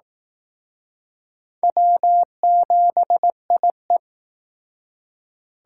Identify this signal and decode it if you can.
Morse code, heard as a keyed tone, about 20 wpm, 700 Hz. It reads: W7IE